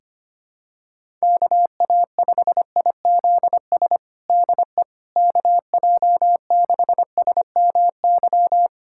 Morse code, heard as a keyed tone, about 25 wpm, 700 Hz. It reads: KA5IZS DE KJ6SMY
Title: USA Callsign DE Callsign - 6 Characters - 25wpm